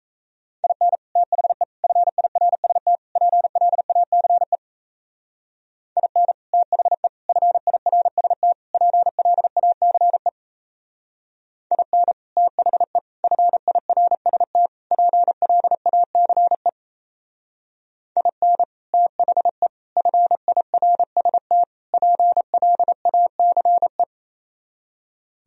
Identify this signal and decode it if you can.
Morse code, heard as a keyed tone, about 34 wpm, 700 Hz. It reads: IN THE FIRST PLACE IN THE FIRST PLACE IN THE FIRST PLACE IN THE FIRST PLACE
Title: Speed-Racing - Sets of 4 Words 28wpm